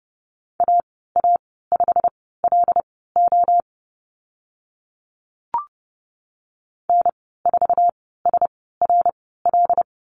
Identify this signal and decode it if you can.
Morse code, heard as a keyed tone, about 30 wpm, spaced as for 17 wpm, 700 Hz. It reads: AA5LO N4SRL
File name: Lesson 15g - Number 5 - Call Signs 30wpm